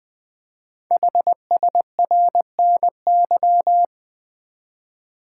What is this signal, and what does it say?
Morse code, heard as a keyed tone, about 20 wpm, 700 Hz. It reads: HSRNY
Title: Code Groups - 5 Letters - 20wpm